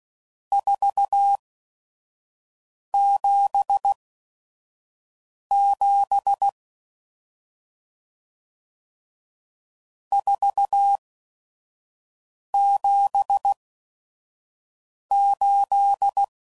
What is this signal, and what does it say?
Morse code, heard as a keyed tone, about 16 wpm, spaced as for 5 wpm, 800 Hz. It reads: 477 478
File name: CW Code Groups @5WPM / RST's / Various RST No's